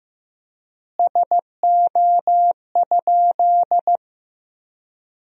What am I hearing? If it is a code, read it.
Morse code, heard as a keyed tone, about 15 wpm, 700 Hz. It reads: SO?
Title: 4x Word-Spacing - Sentences from Top 200 Words - 15wpm